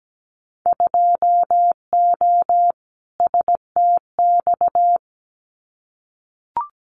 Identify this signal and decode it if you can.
Morse code, heard as a keyed tone, about 17 wpm, 700 Hz. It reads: 2O STX
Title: Winter Field Day Exchanges 17wpm